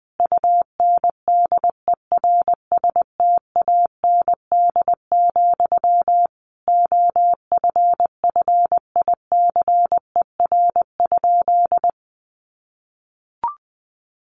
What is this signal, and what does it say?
Morse code, heard as a keyed tone, about 20 wpm, 700 Hz. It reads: UNDERSTAND, OFFICER?
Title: Sentences from Top 1200 Words 20wpm